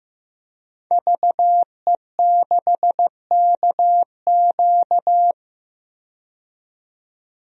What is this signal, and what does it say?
Morse code, heard as a keyed tone, about 15 wpm, 700 Hz. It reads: VE6KQ
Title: DXCC-100 Call Signs - 5 Characters - 15wpm